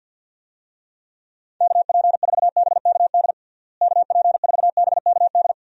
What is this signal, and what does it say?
Morse code, heard as a keyed tone, about 52 wpm, 700 Hz. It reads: KP4BKD KP4BKD
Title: DXCC-100 Call Signs - 6 Characters - Repeated 3x - 50wpm